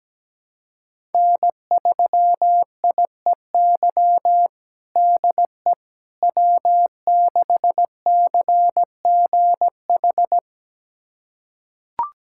Morse code, heard as a keyed tone, about 17 wpm, 700 Hz. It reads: N3IEY DE W6CGH
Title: USA Callsign DE Callsign - 5 Characters - 17wpm